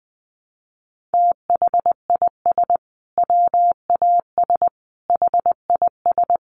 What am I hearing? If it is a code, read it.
Morse code, heard as a keyed tone, about 19 wpm, 700 Hz. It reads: THIS WAS HIS
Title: Sentences from Top 1300 Words 20wpm